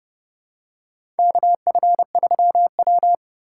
Morse code, heard as a keyed tone, about 30 wpm, 700 Hz. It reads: KF3W
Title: USA Call Signs - 4 Characters - 3x Character Spacing - 30wpm